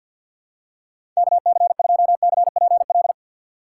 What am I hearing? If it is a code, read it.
Morse code, heard as a keyed tone, about 46 wpm, 700 Hz. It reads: KC2CPL
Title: USA Call Signs - 6 Characters - 3x Character Spacing - 50wpm